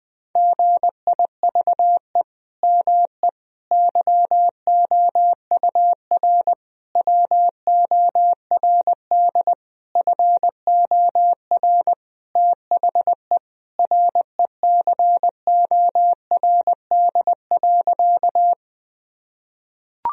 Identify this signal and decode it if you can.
Morse code, heard as a keyed tone, about 20 wpm, 700 Hz. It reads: GIVE ME YOUR WORD FOR THE RECORD.
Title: Sentences from Top 1000 Words - Review Part 3 - 20wpm